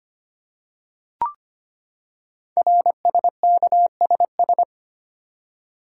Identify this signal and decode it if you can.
Morse code, heard as a keyed tone, about 25 wpm, 700 Hz. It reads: RSKSS
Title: Code Groups - 5 Letters - 25wpm